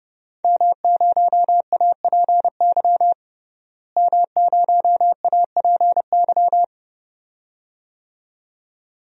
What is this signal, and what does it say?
Morse code, heard as a keyed tone, about 32 wpm, 700 Hz. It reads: M0APY M0APY
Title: DXCC-100 Call Signs - 5 Characters - Repeated 2x - 30wpm